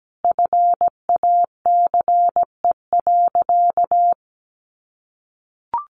Morse code, heard as a keyed tone, about 17 wpm, 700 Hz. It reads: FACE.
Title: Sentences from Top 700 Words 17wpm